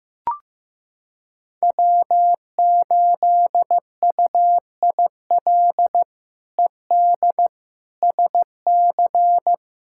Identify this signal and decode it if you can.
Morse code, heard as a keyed tone, about 16 wpm, 700 Hz. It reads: W8UIL ED SC
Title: K1USN SST QSO Practice - 15wpm